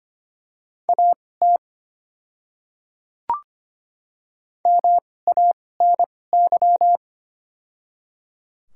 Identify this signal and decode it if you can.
Morse code, heard as a keyed tone, about 25 wpm, spaced as for 18 wpm, 700 Hz. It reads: AT MANY
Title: 2x Character Spacing - Top 100 Words - 25wpm